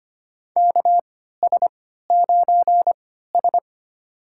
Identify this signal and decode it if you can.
Morse code, heard as a keyed tone, about 25 wpm, spaced as for 14 wpm, 700 Hz. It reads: KS9S
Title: USA Call Signs - 4 Characters - 3x Character Spacing - 25wpm